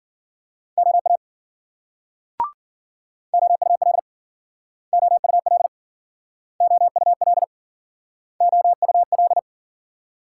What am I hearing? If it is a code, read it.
Morse code, heard as a keyed tone, about 52 wpm, 700 Hz. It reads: OA OUL OUL OUL OUL